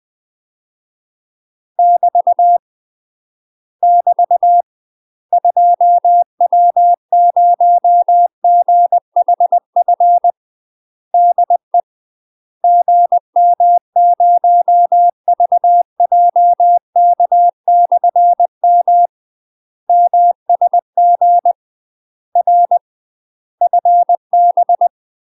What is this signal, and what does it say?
Morse code, heard as a keyed tone, about 20 wpm, 700 Hz. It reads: <BT> <BT> 2W0GHF DE GM0VJK/M MSG R FB
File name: UK MASTER QSOs TRAINING/20WPM